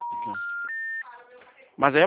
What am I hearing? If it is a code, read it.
Special information tone, with speech.
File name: out- -2007- - - .45186